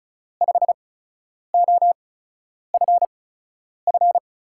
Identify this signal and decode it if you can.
Morse code, heard as a keyed tone, about 36 wpm, spaced as for 10 wpm, 700 Hz. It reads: 5OFF
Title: USA Call Signs - 8x Character Spacing - 35wpm